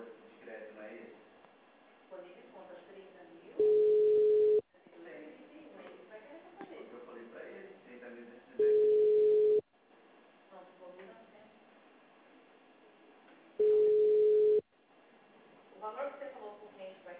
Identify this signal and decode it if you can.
Ringback tone (Europe, 425 Hz), with speech.